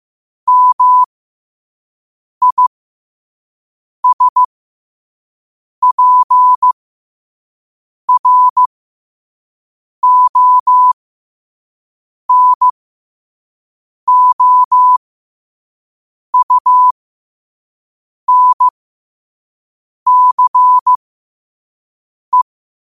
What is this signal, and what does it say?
Morse code, heard as a keyed tone, about 15 wpm, spaced as for 5 wpm, 1000 Hz. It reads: MISPRONOUNCE